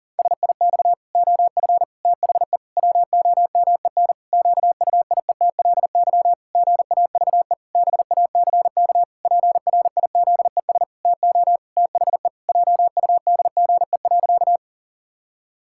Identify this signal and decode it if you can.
Morse code, heard as a keyed tone, about 37 wpm, 700 Hz. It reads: SIX OF THE WOMEN QUIETLY GAVE BACK PRIZES TO THE JUDGE.
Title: Pangram Challenge v2 - 40wpm